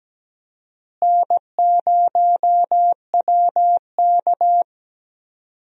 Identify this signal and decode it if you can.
Morse code, heard as a keyed tone, about 17 wpm, 700 Hz. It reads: N0WK
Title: USA Call Signs - 4 Characters - Repeated 2x - 17wpm